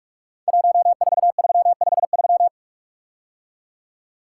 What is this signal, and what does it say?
Morse code, heard as a keyed tone, about 44 wpm, 700 Hz. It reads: 14353
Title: Code Groups - 5 Letter:Numbers - 45wpm